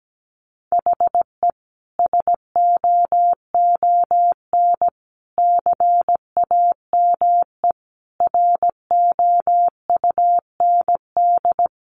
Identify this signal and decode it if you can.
Morse code, heard as a keyed tone, about 17 wpm, 700 Hz. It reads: HE SOON CAME ROUND
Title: Sentences from Top 600 Words 17wpm